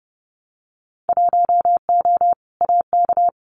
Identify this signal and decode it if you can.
Morse code, heard as a keyed tone, about 30 wpm, 700 Hz. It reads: 1O AK
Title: Winter Field Day Exchanges Encore 30wpm